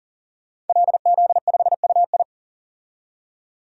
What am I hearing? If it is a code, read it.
Morse code, heard as a keyed tone, about 40 wpm, 700 Hz. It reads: LZ5UI
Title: DXCC-100 Call Signs - 5 Characters - 40wpm